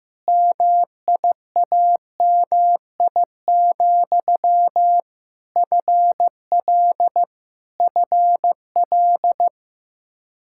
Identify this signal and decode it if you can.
Morse code, heard as a keyed tone, about 15 wpm, 700 Hz. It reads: MIAMI, FL FL
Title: Practice QSO - Exchange 1 - Beginning - 15wpm